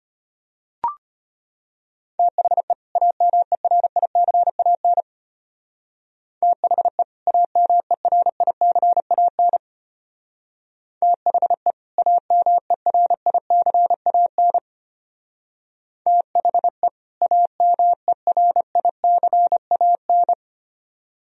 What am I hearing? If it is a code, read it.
Morse code, heard as a keyed tone, about 32 wpm, 700 Hz. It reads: THE AMERICAN THE AMERICAN THE AMERICAN THE AMERICAN